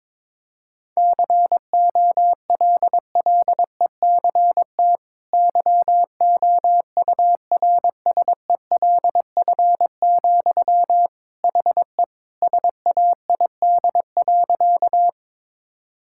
Morse code, heard as a keyed tone, about 22 wpm, 700 Hz. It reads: COLLECT YOURSELF, HE SAID.
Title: Sentences from Top 1600 Words 22wpm